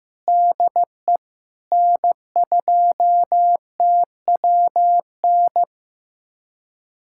Morse code, heard as a keyed tone, about 15 wpm, 700 Hz. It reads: DE N2TWN
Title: USA Callsign DE Callsign - 5 Characters - 15wpm